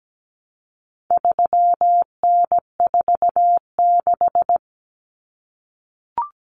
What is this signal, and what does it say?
Morse code, heard as a keyed tone, about 17 wpm, 700 Hz. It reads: 3N46
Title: Common Cut Numbers - 4-Digit 17wpm